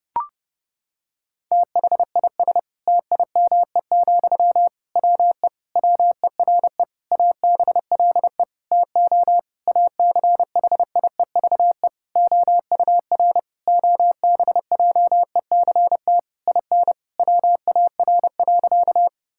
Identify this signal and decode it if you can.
Morse code, heard as a keyed tone, about 30 wpm, 700 Hz. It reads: THIS TIME, WE WERE ABLE TO ACHIEVE OUR OBJECT IN WAR.
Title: Sentences from Top 1000 Words - Review Part 6 - 30wpm